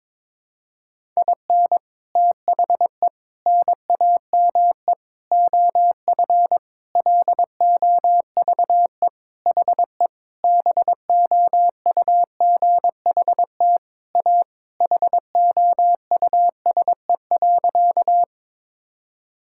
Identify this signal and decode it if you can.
Morse code, heard as a keyed tone, about 22 wpm, 700 Hz. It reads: IN THE NAME OF LOVE HE BOUGHT A HOUSE.
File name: practice / 22wpm / Sentences with 4-Word Phrases - 22wpm